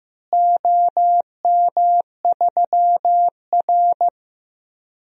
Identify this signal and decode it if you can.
Morse code, heard as a keyed tone, about 15 wpm, 700 Hz. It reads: OM3R
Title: DXCC-100 Call Signs - 4 Characters - Repeated 2x - 15wpm